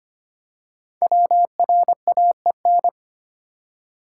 Morse code, heard as a keyed tone, about 25 wpm, 700 Hz. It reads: WRAEN
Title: Code Groups - 5 Letters - 25wpm